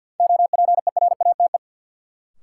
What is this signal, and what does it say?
Morse code, heard as a keyed tone, about 53 wpm, 700 Hz. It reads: OPERATE
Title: Sentences from Top 1500 Words - New Word Drill Alt - 50wpm